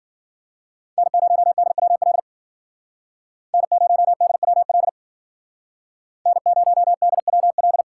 Morse code, heard as a keyed tone, about 52 wpm, 700 Hz. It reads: N0DWL N0DWL N0DWL